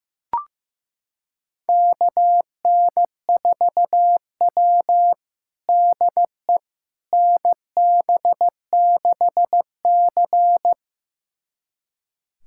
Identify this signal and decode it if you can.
Morse code, heard as a keyed tone, about 16 wpm, 700 Hz. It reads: KN4W DE NB6C
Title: USA Callsign DE Callsign - 4 Characters - 15wpm